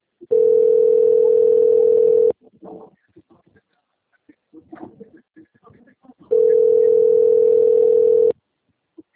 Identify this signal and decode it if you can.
Ringback tone (North America), 440 + 480 Hz.